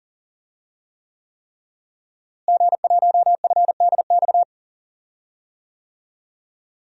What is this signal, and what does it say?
Morse code, heard as a keyed tone, about 40 wpm, 700 Hz. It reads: G1FDX
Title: DXCC-100 Call Signs - 5 Characters - Repeated 3x - 40wpm